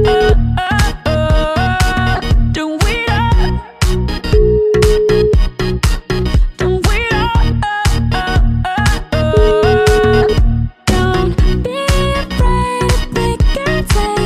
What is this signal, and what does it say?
Ringback tone (Europe, 425 Hz), with noise.